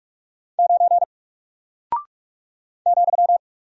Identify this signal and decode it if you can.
Morse code, heard as a keyed tone, about 44 wpm, 700 Hz. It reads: 9 ,
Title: Single Character (Letter:Number:Punctuation:Prosign) 45wpm